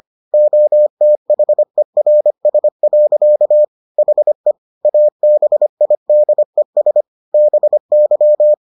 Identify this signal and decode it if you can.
Morse code, heard as a keyed tone, about 25 wpm, 600 Hz. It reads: OTHERS. HE ABIDES BY